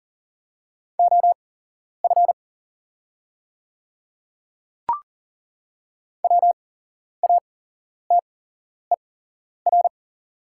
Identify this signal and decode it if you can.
Morse code, heard as a keyed tone, about 38 wpm, spaced as for 11 wpm, 700 Hz. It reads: OF WATER
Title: 8x Character Spacing - Top 100 Words - 40wpm